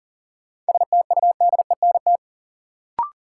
Morse code, heard as a keyed tone, about 40 wpm, 700 Hz. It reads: STUDENT
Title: Sentences from Top 1000 Words - New Word Drill - 40wpm